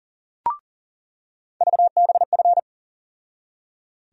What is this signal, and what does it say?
Morse code, heard as a keyed tone, about 40 wpm, 700 Hz. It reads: VBF
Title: USA Call Sign Postfixes - 3 Letters 40wpm